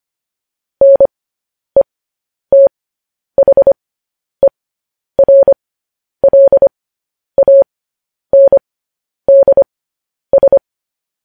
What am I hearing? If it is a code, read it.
Morse code, heard as a keyed tone, about 25 wpm, spaced as for 10 wpm, 550 Hz. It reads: NETHERLANDS
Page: news headlines in morse code at 10 wpm